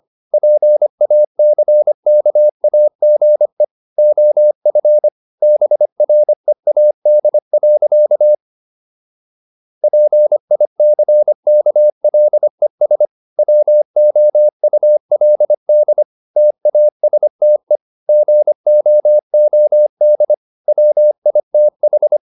Morse code, heard as a keyed tone, about 25 wpm, 600 Hz. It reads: PACKAGE OF BREAD. PICKLES WOULD TASTE GOOD WITH